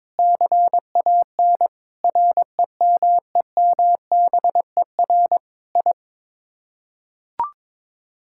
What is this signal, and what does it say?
Morse code, heard as a keyed tone, about 21 wpm, 700 Hz. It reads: CAN REMEMBER I
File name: practice / 22wpm / Sets of 8 Words - 22wpm